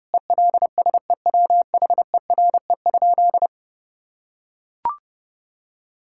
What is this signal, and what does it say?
Morse code, heard as a keyed tone, about 30 wpm, 700 Hz. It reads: ELSEWHERE?
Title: Sentences from Top 1600 Words 30wpm